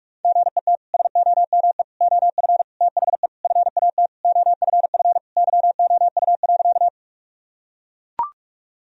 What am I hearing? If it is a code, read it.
Morse code, heard as a keyed tone, about 41 wpm, 700 Hz. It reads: GET SOME OF THE FAT OFF YOU.
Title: Sentences from Top 1000 Words - Review Part 4 - 45wpm